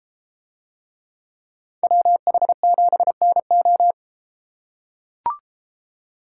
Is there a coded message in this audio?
Yes: Morse code from 1.8 to 3.9 s, heard as a keyed tone, about 33 wpm, 700 Hz. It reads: WH7NO